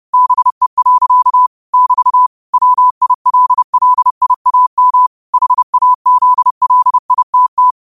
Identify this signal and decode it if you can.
Morse code, heard as a keyed tone, about 30 wpm, 1000 Hz. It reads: BE. <BT> WILLIAM HAZLITT